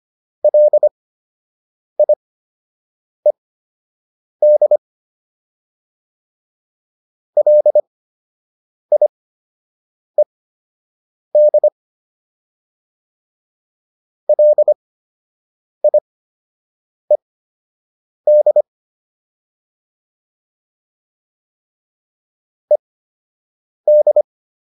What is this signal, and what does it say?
Morse code, heard as a keyed tone, about 25 wpm, spaced as for 7 wpm, 600 Hz. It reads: LIED LIED LIED ED